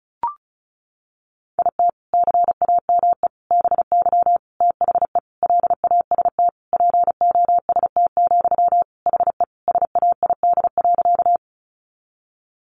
Morse code, heard as a keyed tone, about 34 wpm, 700 Hz. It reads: IT CAME BY THE LAST POST, HE SAID.
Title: Sentences from Top 1000 Words 35wpm